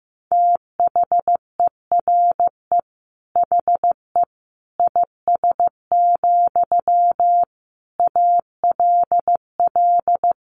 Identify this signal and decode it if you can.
Morse code, heard as a keyed tone, about 15 wpm, 700 Hz. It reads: THERE HE IS, ALL
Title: Sentences from Top 900 Words 15wpm